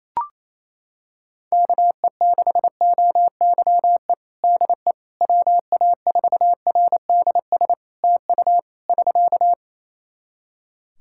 Morse code, heard as a keyed tone, about 28 wpm, 700 Hz. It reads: KE6OYE DE WA4RDS TU <SK>
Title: Practice QSO - Exchange 4 - Ending - 28wpm